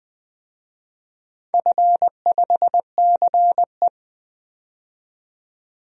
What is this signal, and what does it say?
Morse code, heard as a keyed tone, about 20 wpm, 700 Hz. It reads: F5CE
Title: DXCC-100 Call Signs - 4 Characters - 20wpm